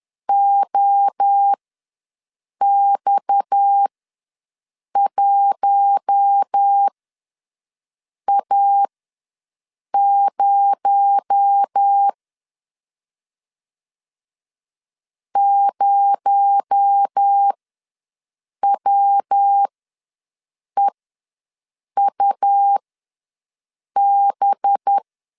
Morse code, heard as a keyed tone, about 10 wpm, spaced as for 6 wpm, 800 Hz. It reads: OX1A0 0WEUB
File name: LEZIONI / 6 PKX0 / 6 PKX0 ws 10wpm 4220